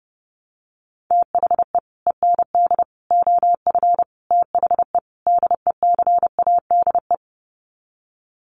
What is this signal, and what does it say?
Morse code, heard as a keyed tone, about 30 wpm, 700 Hz. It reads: THE END OF THE DECADE
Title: Sets of 5 Words Encore 30wpm